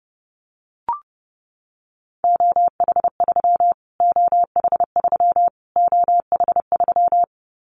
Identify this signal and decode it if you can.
Morse code, heard as a keyed tone, about 30 wpm, 700 Hz. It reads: OH3 OH3 OH3
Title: Callsign Prefixes for Top 25 DXCC - Repeated 3x - 30wpm